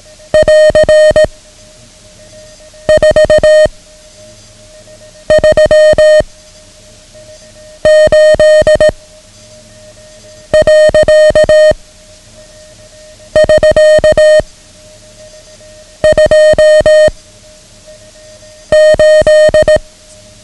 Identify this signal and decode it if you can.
Morse code, heard as a keyed tone, about 18 wpm, 620 Hz. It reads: <AR> 4 3 8 . <SK> 2 8
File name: ham radio / CodeQuick / Code Quick 09